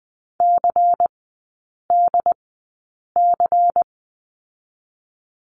Morse code, heard as a keyed tone, about 20 wpm, spaced as for 8 wpm, 690 Hz. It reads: CDC
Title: 093 Lesson 16e - Letter c - Three Characters 20wpm